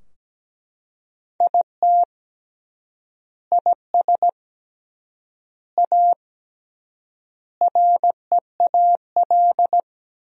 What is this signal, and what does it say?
Morse code, heard as a keyed tone, about 17 wpm, 700 Hz. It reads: IT IS A REAL
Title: 3x Word-Spacing - Sentences from Top 500 Words - 17wpm